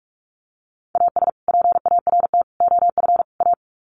Morse code, heard as a keyed tone, about 44 wpm, 700 Hz. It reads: AS PART OF A
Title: Sets of 4 Words 45wpm